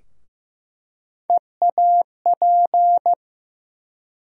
Morse code, heard as a keyed tone, about 15 wpm, 700 Hz. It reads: EAP
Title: USA Call Sign Postfixes - 3 Letters 15wpm